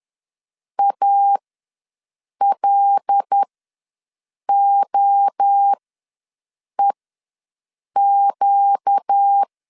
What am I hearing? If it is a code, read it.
Morse code, heard as a keyed tone, about 10 wpm, spaced as for 6 wpm, 800 Hz. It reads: ALOEQ
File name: LEZIONI / 5 LBVC / 5 L ws 10wpm 3636